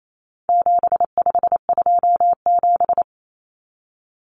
Morse code, heard as a keyed tone, about 28 wpm, 700 Hz. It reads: 7527
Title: Numbers - 4-Digit 28wpm